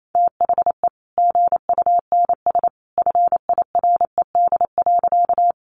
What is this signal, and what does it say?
Morse code, heard as a keyed tone, about 28 wpm, 700 Hz. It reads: THE GUNS FIRED.
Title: A Sound Of Thunder 28wpm